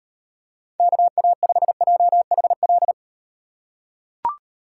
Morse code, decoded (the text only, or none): KA5JHL